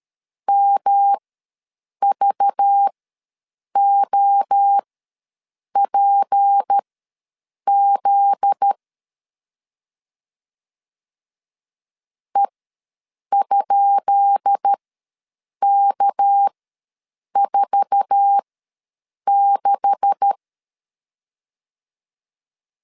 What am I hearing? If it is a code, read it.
Morse code, heard as a keyed tone, about 13 wpm, spaced as for 7 wpm, 800 Hz. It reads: MVOPZ E?K46
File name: LEZIONI / 10 SIGNS / 10B INT ws 12wpm 3527